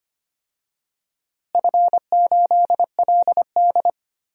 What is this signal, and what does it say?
Morse code, heard as a keyed tone, about 25 wpm, 700 Hz. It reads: F8LD
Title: DXCC-100 Call Signs - 4 Characters - Repeated 3x - 25wpm